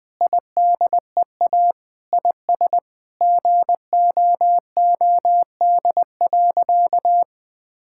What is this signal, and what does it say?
Morse code, heard as a keyed tone, about 20 wpm, 700 Hz. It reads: IDEA IS GOOD.
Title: Sentences with 5-Word Phrases - Spoken Beginning - 20wpm